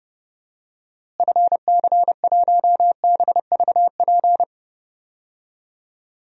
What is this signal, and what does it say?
Morse code, heard as a keyed tone, about 30 wpm, 700 Hz. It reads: FC1BVP
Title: DXCC-100 Call Signs - 6 Characters - Repeated 3x - 30wpm